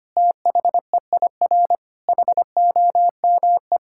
Morse code, heard as a keyed tone, about 25 wpm, 700 Hz. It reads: THEIR HOME
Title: Sentences from Top 1000 Words - Review Part 5 - 25wpm